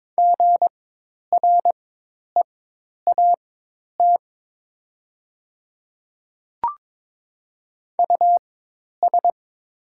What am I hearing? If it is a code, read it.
Morse code, heard as a keyed tone, about 22 wpm, spaced as for 10 wpm, 700 Hz. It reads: GREAT US
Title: 4x Character Spacing - Top 300 Words - 22wpm